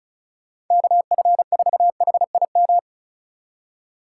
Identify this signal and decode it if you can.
Morse code, heard as a keyed tone, about 44 wpm, 700 Hz. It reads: KF4HIM